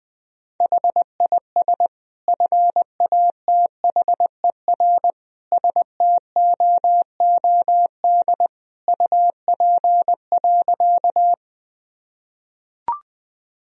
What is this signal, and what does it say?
Morse code, heard as a keyed tone, about 20 wpm, 700 Hz. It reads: HIS FATHER STOOD UP.
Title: Sentences from Top 1000 Words - Review Part 2 - 20wpm